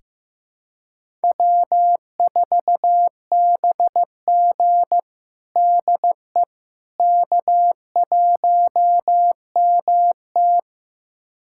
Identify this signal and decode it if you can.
Morse code, heard as a keyed tone, about 15 wpm, 700 Hz. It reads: W4BG DE K1MT
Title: USA Callsign DE Callsign - 4 Characters - 15wpm